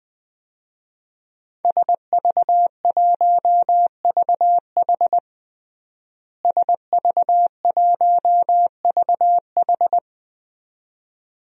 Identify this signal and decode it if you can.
Morse code, heard as a keyed tone, about 20 wpm, 700 Hz. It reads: SV1VH SV1VH
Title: DXCC-100 Call Signs - 5 Characters - Repeated 2x - 20wpm